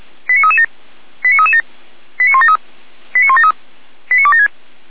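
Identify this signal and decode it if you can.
Five-tone selective call (ZVEI), heard as quick runs of five tones, tones of about 70 ms. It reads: 89337 89337 89173 89173 89276
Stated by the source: ZVEI five-tone selective call received on SDR.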